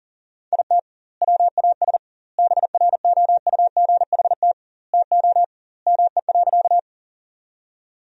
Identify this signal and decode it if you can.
Morse code, heard as a keyed tone, about 40 wpm, 700 Hz. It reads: IT WAS BROUGHT TO ME.
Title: 2x Word-Spacing - Sentences from Top 500 Words - 40wpm